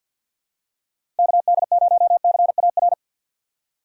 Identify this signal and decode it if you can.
Morse code, heard as a keyed tone, about 53 wpm, 700 Hz. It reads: KD0CAR
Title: USA Call Signs - 6 Characters - 8x Character Spacing - 50wpm